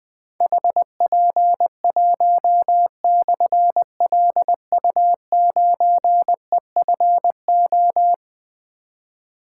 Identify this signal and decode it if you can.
Morse code, heard as a keyed tone, about 20 wpm, 700 Hz. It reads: HP1/LU9EFO
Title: Difficult Call Signs 20wpm